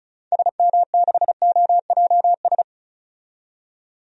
Morse code, heard as a keyed tone, about 34 wpm, 700 Hz. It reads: SM6OJS
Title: DXCC-100 Call Signs - 6 Characters - 35wpm